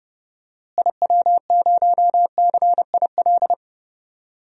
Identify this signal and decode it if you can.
Morse code, heard as a keyed tone, about 30 wpm, 700 Hz. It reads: IW0CIL
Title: DXCC-100 Call Signs - 6 Characters - Repeated 2x - 30wpm